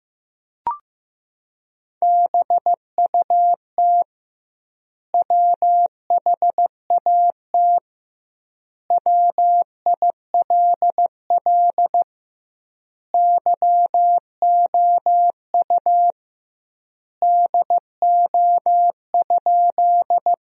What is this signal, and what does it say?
Morse code, heard as a keyed tone, about 15 wpm, 700 Hz. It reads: BUT WHAT WILL YOU DO?
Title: 2x Word-Spacing - Sentences from Top 100 Words - 15wpm